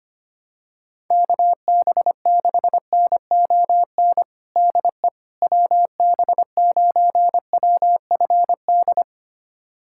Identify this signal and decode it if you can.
Morse code, heard as a keyed tone, about 25 wpm, 700 Hz. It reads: KB6NON DE WB9WFD
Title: USA Callsign DE Callsign 25wpm